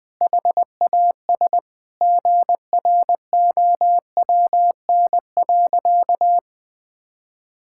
Morse code, heard as a keyed tone, about 20 wpm, 700 Hz. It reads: HAS GROWN.
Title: Sentences with 5-Word Phrases - Spoken Beginning - 20wpm